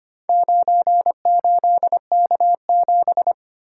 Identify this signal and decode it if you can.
Morse code, heard as a keyed tone, about 25 wpm, 700 Hz. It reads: 98K7
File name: practice / 25wpm / Four Letter-Number Combinations - 25wpm